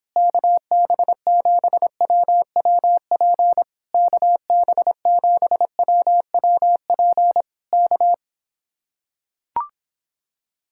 Morse code, heard as a keyed tone, about 26 wpm, 700 Hz. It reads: KB7WWP KB7WWP K